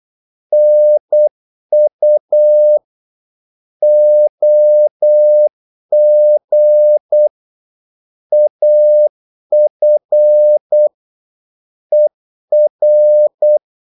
Morse code, heard as a keyed tone, about 8 wpm, 600 Hz. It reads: NU OG AF ER
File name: Korte ord DK 8wpm